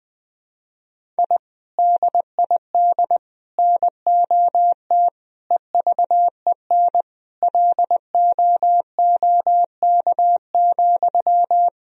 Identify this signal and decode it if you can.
Morse code, heard as a keyed tone, about 20 wpm, 700 Hz. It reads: I DID NOT EVEN LOOK,
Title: Sentences from Top 1000 Words - Review Part 1 - 20wpm